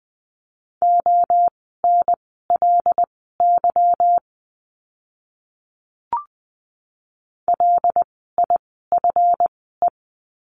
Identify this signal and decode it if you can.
Morse code, heard as a keyed tone, about 20 wpm, spaced as for 15 wpm, 700 Hz. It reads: ONLY LIFE
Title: 206 Lesson 33f - Letter j - Words 20wpm